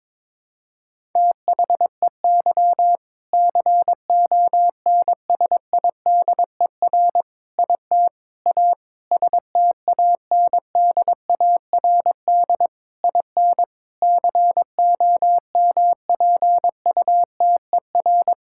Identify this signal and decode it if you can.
Morse code, heard as a keyed tone, about 22 wpm, 700 Hz. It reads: THEY CONSIDER IT A STANDARD IN COMPUTER
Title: Sentences from Top 1000 Words - Review Part 6 - 22wpm